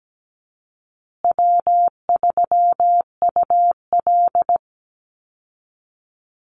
Morse code, heard as a keyed tone, about 17 wpm, 700 Hz. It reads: W3UL
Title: Easy Call Signs Repeated 3x - 17wpm